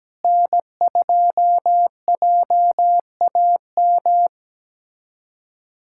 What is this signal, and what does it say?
Morse code, heard as a keyed tone, about 17 wpm, 700 Hz. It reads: N2JAM
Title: USA Callsign DE Callsign - 5 Characters - 17wpm